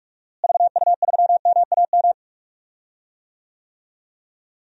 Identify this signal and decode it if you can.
Morse code, heard as a keyed tone, about 42 wpm, 700 Hz. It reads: VU3MAM
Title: DXCC-100 Call Signs - 6 Characters - Repeated 2x - 45wpm